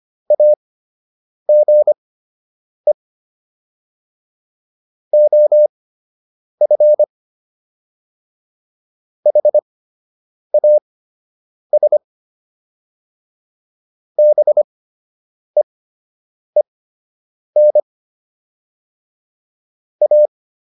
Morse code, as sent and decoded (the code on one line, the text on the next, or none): .- --. / --- ..-. / .... .- ... / -... . -. / .-
AGE OF HAS BEEN A